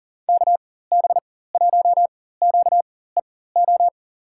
Morse code, heard as a keyed tone, about 40 wpm, spaced as for 19 wpm, 700 Hz. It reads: KB1QEO
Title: USA Call Signs - 4x Character Spacing - 40wpm